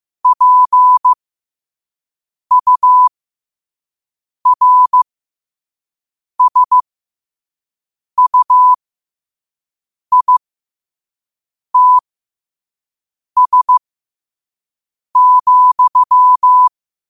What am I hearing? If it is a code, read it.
Morse code, heard as a keyed tone, about 15 wpm, spaced as for 5 wpm, 1000 Hz. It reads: PURSUITS,
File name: qotdfiles / 5wpm